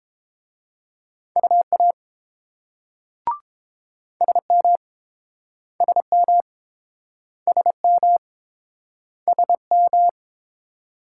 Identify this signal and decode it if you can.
Morse code, heard as a keyed tone, about 31 wpm, 700 Hz. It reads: UA SM SM SM SM